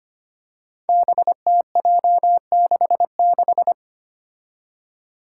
Morse code, heard as a keyed tone, about 25 wpm, 700 Hz. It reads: BTJ66
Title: Five Challenging Letter-Number Combinations 25wpm